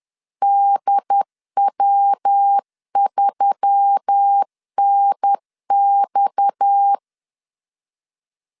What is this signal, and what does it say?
Morse code, heard as a keyed tone, about 10 wpm, 800 Hz. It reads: DW3NX